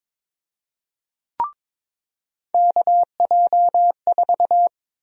Morse code, heard as a keyed tone, about 22 wpm, 700 Hz. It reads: KJ4
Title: USA Call Sign Prefixes - 22wpm